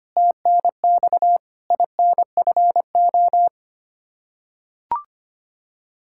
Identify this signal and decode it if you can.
Morse code, heard as a keyed tone, about 25 wpm, 700 Hz. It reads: TNX INFO
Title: Practice QSO - Exchange 3 - More Info - 25wpm